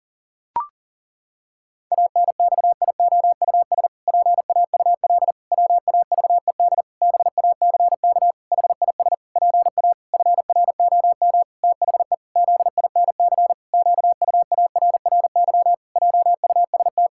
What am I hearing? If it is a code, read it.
Morse code, heard as a keyed tone, about 40 wpm, 700 Hz. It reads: ANXIOUS PAUL WAVED BACK HIS PA FROM THE ZINC QUARRY JUST
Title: Pangram Challenge v2 - 40wpm